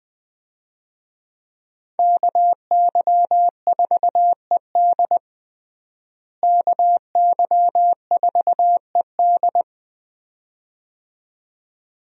Morse code, heard as a keyed tone, about 20 wpm, 700 Hz. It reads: KY4ED KY4ED
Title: USA Call Signs - 5 Characters - Repeated 2x - 20wpm